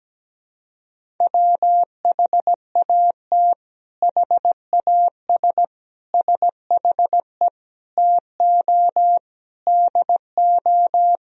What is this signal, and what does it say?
Morse code, heard as a keyed tone, about 17 wpm, 700 Hz. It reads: WHAT HAS SHE TO DO
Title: Sentences from Top 100 Words and Abbreviations 17wpm